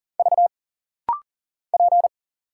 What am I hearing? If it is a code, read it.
Morse code, heard as a keyed tone, about 40 wpm, 700 Hz. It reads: V P
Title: Single Letter-Number Alt - Rapid-Fire 40wpm